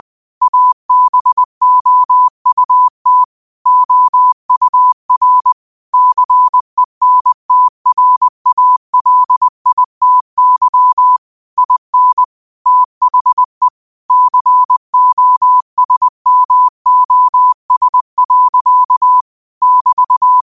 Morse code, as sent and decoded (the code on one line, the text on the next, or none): .- -... --- ..- - / --- ..- .-. / -.-. -. - .-. .- .-.. .. - -.-- / .. -. / - .... / -.-. --- ... -- --- ... .-.-.- / -...-
ABOUT OUR CENTRALITY IN THE COSMOS. <BT>